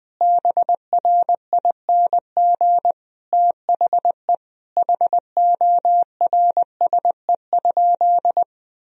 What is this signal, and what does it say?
Morse code, heard as a keyed tone, about 20 wpm, 700 Hz. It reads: BRING THE HORSE?
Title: Sentences from Top 500 Words and Abbreviations 20wpm